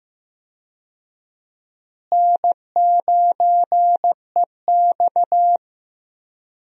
Morse code, heard as a keyed tone, about 15 wpm, 700 Hz. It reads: N9EX
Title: USA Call Signs - 4 Characters - Repeated 3x - 15wpm